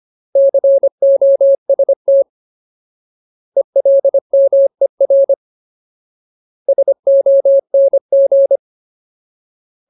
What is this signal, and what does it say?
Morse code, heard as a keyed tone, about 25 wpm, 550 Hz. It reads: COST ELMER SONG
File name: Ultimate Word List / 25WPM Ultimate